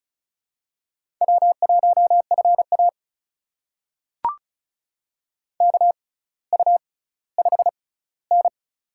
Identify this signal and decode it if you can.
Morse code, heard as a keyed tone, about 36 wpm, spaced as for 18 wpm, 700 Hz. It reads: W1FA KU5N
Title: USA Call Signs - 4 Characters - 6x Character Spacing - 35wpm